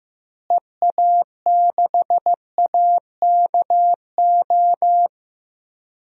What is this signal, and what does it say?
Morse code, heard as a keyed tone, about 15 wpm, 700 Hz. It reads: EA6AKO
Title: DXCC-100 Call Signs - 6 Characters - Repeated 3x - 15wpm